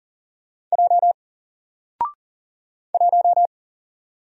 Morse code, heard as a keyed tone, about 40 wpm, 700 Hz. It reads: J 1
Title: Single Character (Letter:Number:Punctuation:Prosign) Alt 40wpm